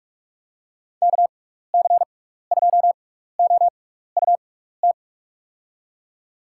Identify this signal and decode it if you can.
Morse code, heard as a keyed tone, about 41 wpm, spaced as for 15 wpm, 700 Hz. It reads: KC2OUT
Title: USA Call Signs - 6 Characters - 6x Character Spacing - 45wpm